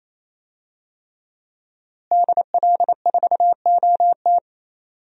Morse code, heard as a keyed tone, about 27 wpm, 700 Hz. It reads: DL4OT
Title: USA Call Signs - 5 Characters - Repeated 3x - 28wpm